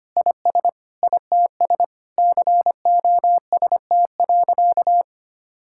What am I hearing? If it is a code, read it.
Morse code, heard as a keyed tone, about 25 wpm, 700 Hz. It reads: IS ITS COST.
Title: Sentences from Top 1000 Words - Review Part 6 - 25wpm